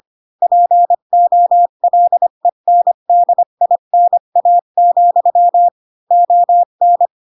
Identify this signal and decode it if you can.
Morse code, heard as a keyed tone, about 25 wpm, 700 Hz. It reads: POLENDINA, ON
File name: Pinocchio / Pinocchio ch2 20 25 ik1hge